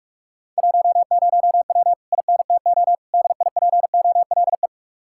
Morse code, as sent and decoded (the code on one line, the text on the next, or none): .---- ----- .-- / .. -. - --- / -.. .. .--. --- .-.. .
10W INTO DIPOLE